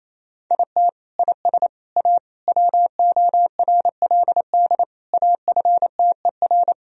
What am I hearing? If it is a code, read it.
Morse code, heard as a keyed tone, about 28 wpm, 700 Hz. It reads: IT IS A WORLD AFTER
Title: Sentences from Top 1000 Words - Review Part 4 - 28wpm